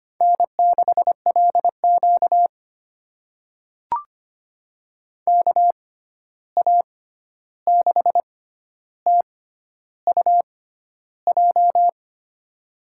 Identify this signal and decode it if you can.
Morse code, heard as a keyed tone, about 25 wpm, spaced as for 11 wpm, 700 Hz. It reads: N6LQ KA6TUJ
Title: USA Call Signs - 6x Character Spacing - 25wpm